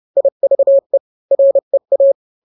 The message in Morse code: IVE REA